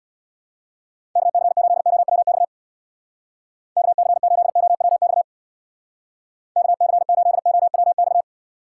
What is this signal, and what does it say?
Morse code, heard as a keyed tone, about 66 wpm, 700 Hz. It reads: KC8OWX KC8OWX KC8OWX